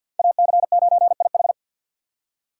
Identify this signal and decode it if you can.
Morse code, heard as a keyed tone, about 52 wpm, 690 Hz. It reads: AC9IH
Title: USA Call Signs - 4x Character Spacing - 50wpm